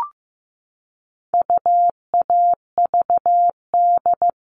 Morse code, heard as a keyed tone, about 15 wpm, 700 Hz. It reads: UAVD
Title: All Cut Numbers - 4-Digit 15wpm